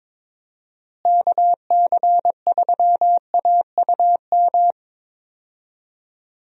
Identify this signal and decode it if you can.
Morse code, heard as a keyed tone, about 22 wpm, 700 Hz. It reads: KC3AUM
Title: USA Call Signs - 6 Characters - Repeated 2x - 22wpm